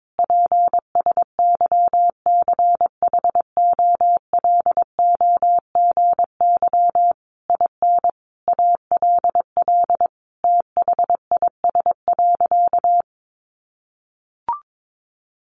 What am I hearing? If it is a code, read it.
Morse code, heard as a keyed tone, about 22 wpm, 700 Hz. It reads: PSYCHOLOGY IN ALL THIS.
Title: Sentences from Top 1400 Words 22wpm